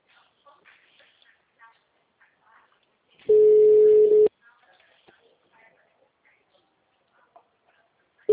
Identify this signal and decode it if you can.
Ringback tone (Europe, 425 Hz), with speech.